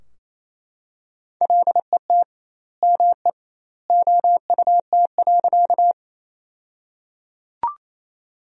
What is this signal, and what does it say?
Morse code, heard as a keyed tone, about 28 wpm, 700 Hz. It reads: LET ME OUT.
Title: 2x Word-Spacing - Sentences from Top 300 Words - 28wpm